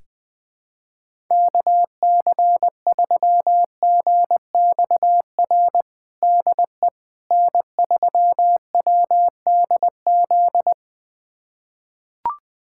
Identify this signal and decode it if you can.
Morse code, heard as a keyed tone, about 20 wpm, 700 Hz. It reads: KC3GXR DE N3WDZ
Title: USA Callsign DE Callsign 20wpm